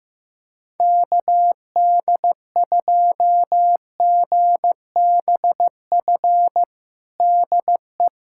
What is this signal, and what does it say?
Morse code, heard as a keyed tone, about 16 wpm, 700 Hz. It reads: KD2GBF DE
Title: USA Callsign DE Callsign 15wpm